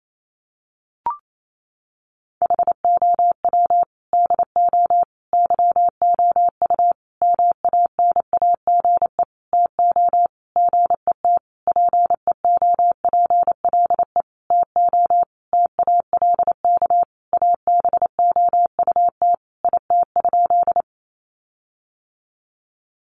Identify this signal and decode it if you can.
Morse code, heard as a keyed tone, about 28 wpm, 700 Hz. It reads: HOW DO YOU MANAGE TO GET PEOPLE TO TALK ABOUT IT?
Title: Sentences from Top 1400 Words 28wpm